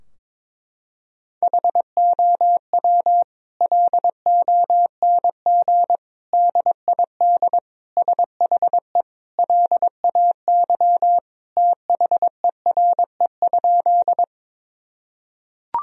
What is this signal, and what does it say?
Morse code, heard as a keyed tone, about 22 wpm, 700 Hz. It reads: HOW LONG DID SHE LAY THERE?
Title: Sentences from Top 1000 Words - Review Part 2 - 22wpm